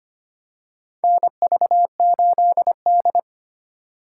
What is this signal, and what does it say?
Morse code, heard as a keyed tone, about 25 wpm, 700 Hz. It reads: NV8D
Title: USA Call Signs - 4 Characters - 8x Character Spacing - 25wpm